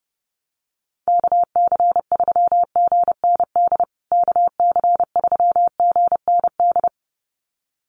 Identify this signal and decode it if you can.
Morse code, heard as a keyed tone, about 30 wpm, 700 Hz. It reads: KC3GND KC3GND
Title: USA Call Signs Repeated 2x 30wpm